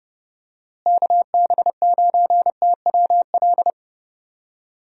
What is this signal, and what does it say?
Morse code, heard as a keyed tone, about 30 wpm, 700 Hz. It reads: KB9TWL
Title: USA Call Signs - 6 Characters - Repeated 2x - 30wpm